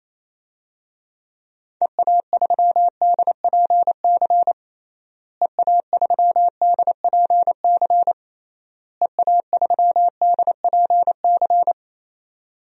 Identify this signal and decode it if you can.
Morse code, heard as a keyed tone, about 28 wpm, 700 Hz. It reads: EA3DPC EA3DPC EA3DPC
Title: DXCC-100 Call Signs - 6 Characters - Repeated 3x - 28wpm